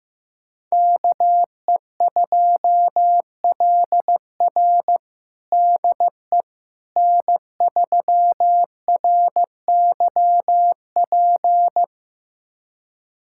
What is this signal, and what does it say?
Morse code, heard as a keyed tone, about 15 wpm, 700 Hz. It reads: KE2LR DE N3RYP
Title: USA Callsign DE Callsign - 5 Characters - 15wpm